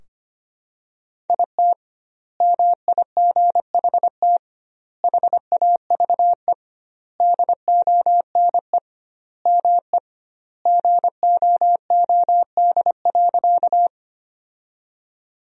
Morse code, heard as a keyed tone, about 25 wpm, 700 Hz. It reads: IT MIGHT HAVE DONE ME GOOD.
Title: 2x Word-Spacing - Sentences from Top 500 Words - 25wpm